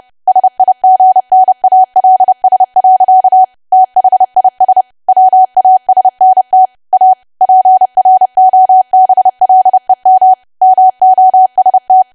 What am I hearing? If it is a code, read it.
Morse code, heard as a keyed tone, about 30 wpm, 750 Hz. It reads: SIGNALS. THIS WASNT A PROBLEM MOST
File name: cwa / W1AW-Level3 / 30WPM